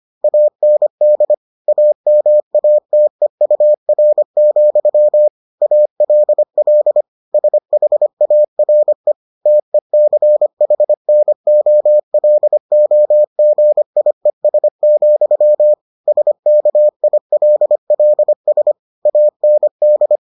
Morse code, heard as a keyed tone, about 25 wpm, 600 Hz. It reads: AND AMATEUR, ALL SHARE TECHNOLOGIES, SKILLS AND